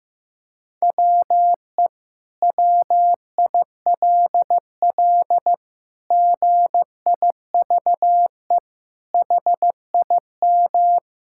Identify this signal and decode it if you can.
Morse code, heard as a keyed tone, about 15 wpm, 700 Hz. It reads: WE WILL GIVE HIM
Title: Sentences from Top 1000 Words - Review Part 5 - 15wpm